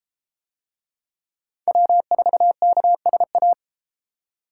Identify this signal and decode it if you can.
Morse code, heard as a keyed tone, about 31 wpm, 700 Hz. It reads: W4KSA